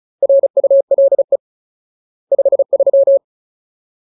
Morse code, heard as a keyed tone, about 34 wpm, 550 Hz. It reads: RULE 53
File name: Ultimate Word List / 35WPM Ultimate